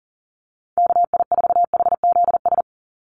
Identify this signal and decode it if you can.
Morse code, heard as a keyed tone, about 40 wpm, 700 Hz. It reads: KI4HZS
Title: USA Call Signs 40wpm